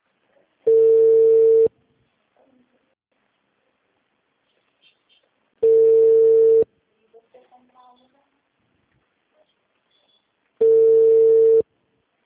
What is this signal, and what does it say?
Ringback tone (China, 450 Hz), with speech.